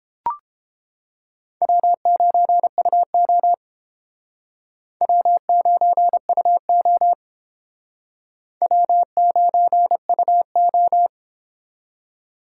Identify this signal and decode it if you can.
Morse code, heard as a keyed tone, about 31 wpm, 700 Hz. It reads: W9UO W9UO W9UO